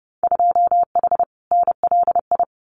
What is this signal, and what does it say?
Morse code, heard as a keyed tone, about 30 wpm, 700 Hz. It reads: 2H NLI
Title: Winter Field Day Exchanges 30wpm